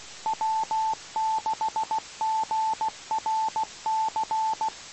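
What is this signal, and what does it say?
Morse code, heard as a keyed tone, about 16 wpm, 880 Hz. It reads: W6GRC
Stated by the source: Morse heard on the 2 m band at 147.105 MHz.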